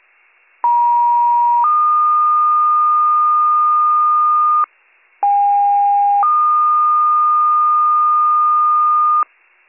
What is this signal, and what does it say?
Two-tone page, heard as a one-second tone, then a three-second tone, about 950 then 1220 Hz, about 810 then 1220 Hz. It is heard from 0.6 to 9.2 s.